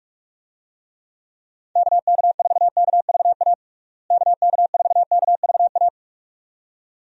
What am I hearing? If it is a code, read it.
Morse code, heard as a keyed tone, about 46 wpm, 700 Hz. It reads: KK4KVA KK4KVA
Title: USA Call Signs - 6 Characters - Repeated 2x - 45wpm